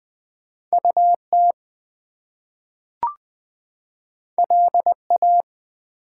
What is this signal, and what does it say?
Morse code, heard as a keyed tone, about 20 wpm, 700 Hz. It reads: UT LA
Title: US States and Canadian Provinces 20wpm